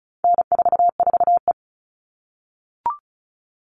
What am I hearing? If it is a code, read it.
Morse code, heard as a keyed tone, about 34 wpm, 700 Hz. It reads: N44E
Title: All Cut Numbers - 4-Digit 35wpm